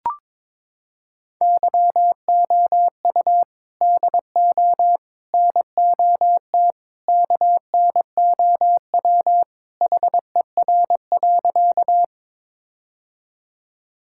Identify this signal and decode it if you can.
Morse code, heard as a keyed tone, about 22 wpm, 700 Hz. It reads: YOU DO NOT KNOW HER.
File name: practice / 22wpm / Sentences with 4-Word Phrases - 22wpm